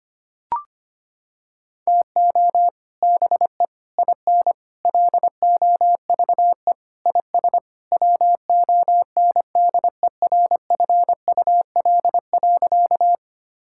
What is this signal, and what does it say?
Morse code, heard as a keyed tone, about 25 wpm, 700 Hz. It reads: TO BE IN LOVE IS WONDERFUL.
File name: practice / 25wpm / Sentences with 3-Word Phrases - 25wpm